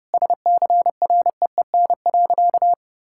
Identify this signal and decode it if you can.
Morse code, heard as a keyed tone, about 30 wpm, 700 Hz. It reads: SCREEN.
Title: Sentences from Top 1000 Words - Review Part 6 - 30wpm